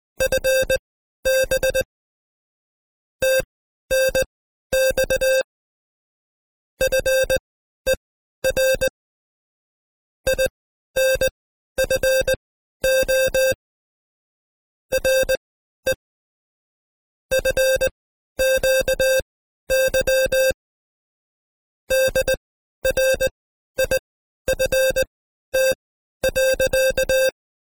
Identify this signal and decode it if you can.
Morse code, heard as a keyed tone, about 19 wpm, spaced as for 12 wpm, 530 Hz. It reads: FB TNX FER INFO RE FQY DRIFT.
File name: Code Quick / AudioDownld2 / Seg132-Stepper5 11WPM